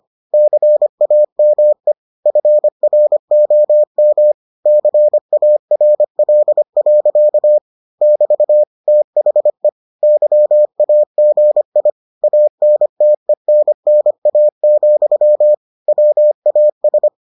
Morse code, heard as a keyed tone, about 25 wpm, 600 Hz. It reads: CAME FROM CARL. <BT> THE YAGI ANTENNA, WAS